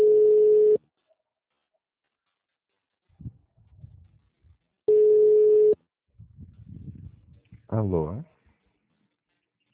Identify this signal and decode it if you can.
Ringback tone (Europe, 425 Hz), with speech.
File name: out- -2041- - - .4688